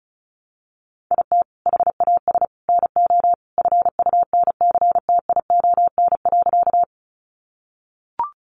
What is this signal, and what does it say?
Morse code, heard as a keyed tone, about 34 wpm, 700 Hz. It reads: IT HAS NO FUNCTION.
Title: Sentences from Top 1100 Words 35wpm